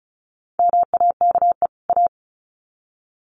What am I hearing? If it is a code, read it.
Morse code, heard as a keyed tone, about 34 wpm, 700 Hz. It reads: MAKE A
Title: Sets of 2 Words 35wpm